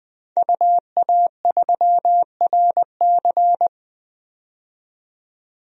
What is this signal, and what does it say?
Morse code, heard as a keyed tone, about 20 wpm, 700 Hz. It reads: UA3RC
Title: DXCC-100 Call Signs - 5 Characters - 20wpm